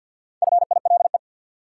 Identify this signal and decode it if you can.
Morse code, heard as a keyed tone, about 51 wpm, 700 Hz. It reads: FILE
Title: Four Letter Words - 50wpm